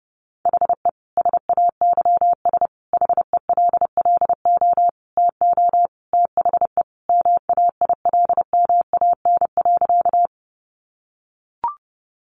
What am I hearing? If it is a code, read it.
Morse code, heard as a keyed tone, about 30 wpm, 700 Hz. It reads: HE SAYS HELLO TO THE MAILMAN.
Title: Easy Phrases 30wpm